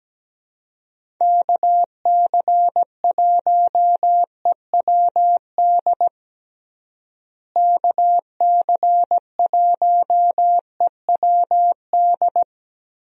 Morse code, heard as a keyed tone, about 17 wpm, 700 Hz. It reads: KC1EWD KC1EWD
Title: USA Call Signs - 6 Characters - Repeated 2x - 17wpm